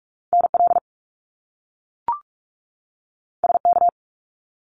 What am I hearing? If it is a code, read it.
Morse code, heard as a keyed tone, about 46 wpm, 700 Hz. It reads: NL SK
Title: Canadian Provinces and Territories 45wpm